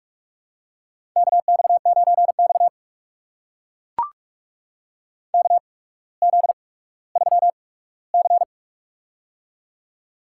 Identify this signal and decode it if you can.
Morse code, heard as a keyed tone, about 44 wpm, spaced as for 19 wpm, 690 Hz. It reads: KX9X KZ3C
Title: USA Call Signs - 4 Characters - 8x Character Spacing - 45wpm